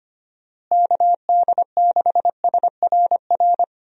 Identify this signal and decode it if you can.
Morse code, heard as a keyed tone, about 25 wpm, 700 Hz. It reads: KD6SRR
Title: USA Call Signs - 6 Characters - Repeated 3x - 25wpm